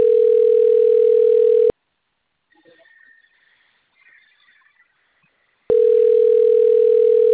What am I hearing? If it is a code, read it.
Ringback tone (North America), 440 + 480 Hz.